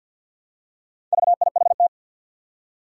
Morse code, heard as a keyed tone, about 46 wpm, 700 Hz. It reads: VIHT
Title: Four Challenging Letter-Number Combinations 50wpm